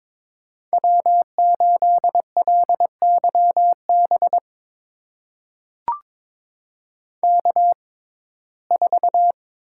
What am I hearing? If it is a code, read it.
Morse code, heard as a keyed tone, about 22 wpm, spaced as for 16 wpm, 700 Hz. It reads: W8LYB K4
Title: USA Call Signs - 5 Characters - 6x Character Spacing - 22wpm